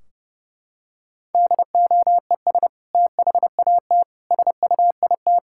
Morse code, heard as a keyed tone, about 28 wpm, 700 Hz. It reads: DOES THAT SUIT
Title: Sentences from Top 1600 Words 30wpm